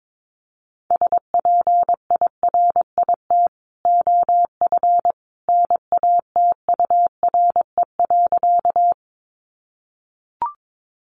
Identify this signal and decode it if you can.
Morse code, heard as a keyed tone, about 22 wpm, 700 Hz. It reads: SPIRIT OF NATURE.
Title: Sentences from Top 1400 Words 22wpm